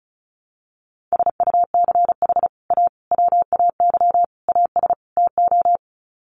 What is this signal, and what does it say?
Morse code, heard as a keyed tone, about 34 wpm, 700 Hz. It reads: SUCH A WAY AS TO
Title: Sets of 5 Words Encore2 35wpm